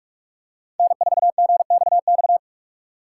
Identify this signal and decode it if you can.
Morse code, heard as a keyed tone, about 46 wpm, 700 Hz. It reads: N4GXX